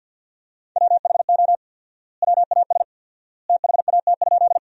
Morse code, heard as a keyed tone, about 50 wpm, 700 Hz. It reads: WHO WAS THAT?
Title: 4x Word-Spacing - Sentences from Top 100 Words - 50wpm